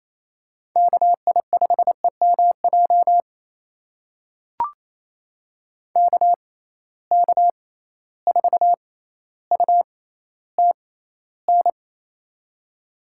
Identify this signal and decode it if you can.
Morse code, heard as a keyed tone, about 28 wpm, spaced as for 14 wpm, 700 Hz. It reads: KI5EMJ KK4UTN
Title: USA Call Signs - 6 Characters - 6x Character Spacing - 28wpm